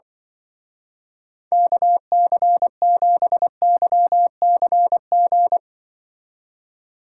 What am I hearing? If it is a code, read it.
Morse code, heard as a keyed tone, about 25 wpm, 700 Hz. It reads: KC7YCG
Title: Speed Racing - USA 2x3 Call Signs 20wpm